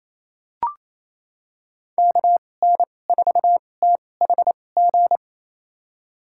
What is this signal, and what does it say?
Morse code, heard as a keyed tone, about 28 wpm, spaced as for 20 wpm, 700 Hz. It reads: KN4THG
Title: USA Call Signs - 6 Characters - 2x Character Spacing - 28wpm